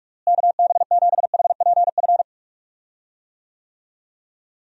Morse code, heard as a keyed tone, about 46 wpm, 700 Hz. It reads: KB7HPF
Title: USA Call Signs - 6 Characters - Repeated 2x - 45wpm